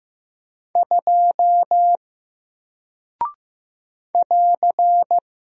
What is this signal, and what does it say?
Morse code, heard as a keyed tone, about 15 wpm, 700 Hz. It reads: 2 <AR>
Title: Single Character (Letter:Number:Punctuation:Prosign) Alt 15wpm